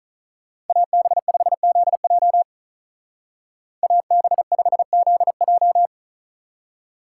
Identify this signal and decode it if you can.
Morse code, heard as a keyed tone, about 43 wpm, 700 Hz. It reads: AB5ZJ AB5ZJ
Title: Speed Racing - USA 2x2 Call Signs 35wpm